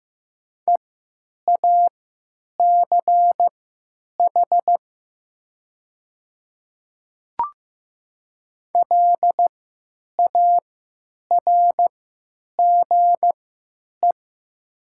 Morse code, heard as a keyed tone, about 15 wpm, spaced as for 9 wpm, 700 Hz. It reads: EACH LARGE
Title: 3x Character Spacing - Top 200 Words - 15wpm